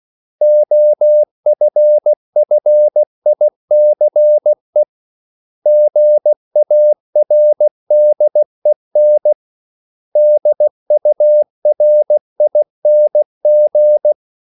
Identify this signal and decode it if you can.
Morse code, heard as a keyed tone, about 16 wpm, 600 Hz. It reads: OFFICE GARDEN DURING
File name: CW / Practice / 500MostCommonEnglishWords16